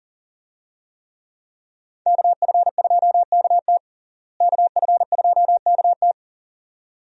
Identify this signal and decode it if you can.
Morse code, heard as a keyed tone, about 37 wpm, 700 Hz. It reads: KF2KT KF2KT
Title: USA Call Signs - 5 Characters - Repeated 2x - 40wpm